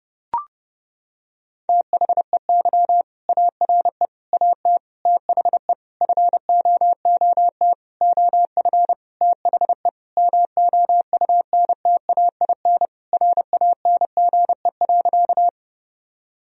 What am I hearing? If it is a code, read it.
Morse code, heard as a keyed tone, about 30 wpm, 700 Hz. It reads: THEY ARE AT THE FOOT OF THE MOUNTAIN RANGE.
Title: Sentences from Top 1000 Words - Review Part 4 - 30wpm